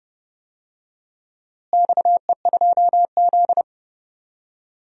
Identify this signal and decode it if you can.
Morse code, heard as a keyed tone, about 30 wpm, 700 Hz. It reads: XE2Z
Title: DXCC-100 Call Signs - 4 Characters - Repeated 2x - 30wpm